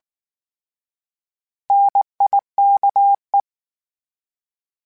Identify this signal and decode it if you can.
Morse code, heard as a keyed tone, about 19 wpm, 800 Hz. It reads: NIKE